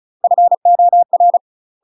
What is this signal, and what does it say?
Morse code, heard as a keyed tone, about 35 wpm, 700 Hz. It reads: FOR
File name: UK MASTER QSOs TRAINING/35WPM